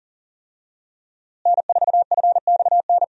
Morse code, heard as a keyed tone, about 40 wpm, 700 Hz. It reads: N4FXN
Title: USA Call Signs - 5 Characters - Repeated 2x - 40wpm